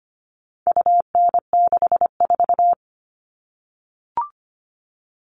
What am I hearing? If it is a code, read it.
Morse code, heard as a keyed tone, about 25 wpm, 700 Hz. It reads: UN64
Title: All Cut Numbers - 4-Digit 25wpm